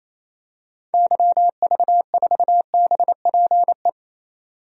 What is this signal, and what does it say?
Morse code, heard as a keyed tone, about 28 wpm, 700 Hz. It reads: YV4BPE
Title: DXCC-100 Call Signs - 6 Characters - 28wpm